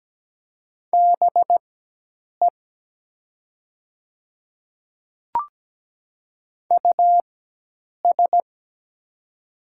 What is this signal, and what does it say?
Morse code, heard as a keyed tone, about 17 wpm, spaced as for 8 wpm, 700 Hz. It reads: BE US
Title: 4x Character Spacing - Top 200 Words - 17wpm